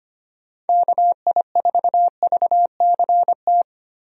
Morse code, heard as a keyed tone, about 25 wpm, 700 Hz. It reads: KI4VCT
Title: USA Call Signs - 6 Characters - Repeated 3x - 25wpm